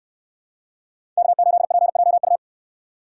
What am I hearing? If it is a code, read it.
Morse code, heard as a keyed tone, about 62 wpm, 700 Hz. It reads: K8PJU